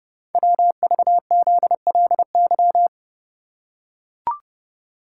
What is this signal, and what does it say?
Morse code, heard as a keyed tone, about 29 wpm, 700 Hz. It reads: WVZLY
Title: Code Groups - 5 Letters - 30wpm